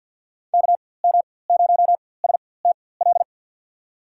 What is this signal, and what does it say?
Morse code, heard as a keyed tone, about 47 wpm, spaced as for 23 wpm, 700 Hz. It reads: KM0STL
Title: USA Call Signs - 6 Characters - 4x Character Spacing - 50wpm